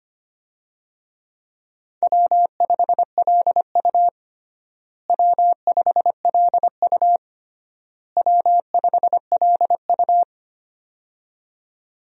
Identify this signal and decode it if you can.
Morse code, heard as a keyed tone, about 25 wpm, 700 Hz. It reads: W5LU W5LU W5LU
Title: USA Call Signs - 4 Characters - Repeated 3x - 25wpm